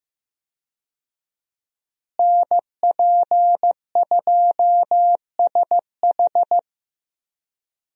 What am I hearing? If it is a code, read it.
Morse code, heard as a keyed tone, about 15 wpm, 700 Hz. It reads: NP2SH
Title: USA Call Signs - 5 Characters - Repeated 2x - 15wpm